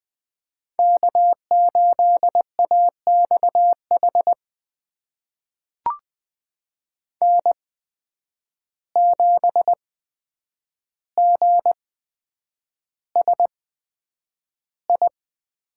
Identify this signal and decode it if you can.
Morse code, heard as a keyed tone, about 20 wpm, spaced as for 9 wpm, 700 Hz. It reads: K8AXH N7GSI
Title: USA Call Signs - 5 Characters - 8x Character Spacing - 20wpm